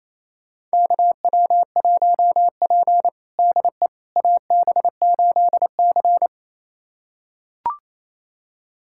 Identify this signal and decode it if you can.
Morse code, heard as a keyed tone, about 27 wpm, 700 Hz. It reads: KW1P DE AB8C
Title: USA Callsign DE Callsign - 4 Characters - 28wpm